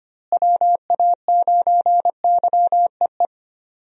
Morse code, heard as a keyed tone, about 25 wpm, 700 Hz. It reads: WA9YEE
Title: USA Call Signs - 6 Characters - Repeated 3x - 25wpm